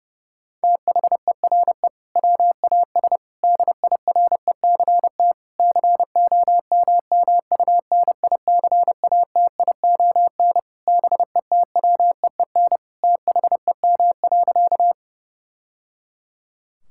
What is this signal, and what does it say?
Morse code, heard as a keyed tone, about 32 wpm, 700 Hz. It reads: THERE WAS DIRECT COMMUNICATION BETWEEN THEM.
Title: Sentences from Top 1000 Words - Review Part 6 - 30wpm